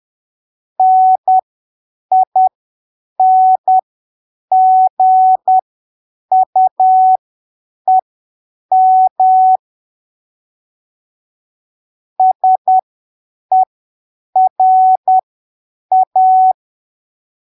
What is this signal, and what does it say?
Morse code, heard as a keyed tone, about 10 wpm, spaced as for 7 wpm, 750 Hz. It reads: NINGUEM SERA
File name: arquivos / texto / texto 28 10ppm